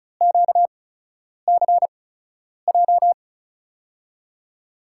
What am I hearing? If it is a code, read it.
Morse code, heard as a keyed tone, about 34 wpm, spaced as for 10 wpm, 700 Hz. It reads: QCJ
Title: USA Call Signs - 5 Characters - 8x Character Spacing - 35wpm